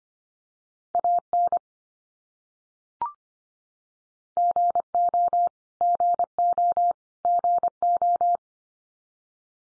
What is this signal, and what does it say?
Morse code, heard as a keyed tone, about 25 wpm, 700 Hz. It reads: AN GO GO GO
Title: Top 100 Words 25wpm x3